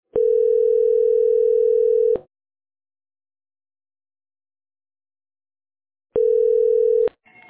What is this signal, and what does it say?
Ringback tone (North America), 440 + 480 Hz.